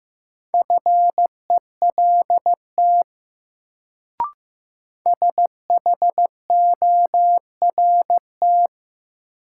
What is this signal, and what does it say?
Morse code, heard as a keyed tone, about 15 wpm, 700 Hz. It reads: FELT SHORT
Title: Sets of 5 Words Encore - New Word Drill - 15wpm